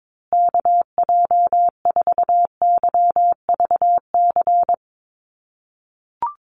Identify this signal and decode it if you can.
Morse code, heard as a keyed tone, about 22 wpm, 700 Hz. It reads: KJ4YVC
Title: USA Call Signs Repeated 2x 22wpm